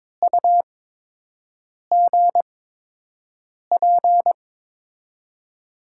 Morse code, heard as a keyed tone, about 22 wpm, spaced as for 6 wpm, 700 Hz. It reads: UGP
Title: USA Call Signs - 8x Character Spacing - 22wpm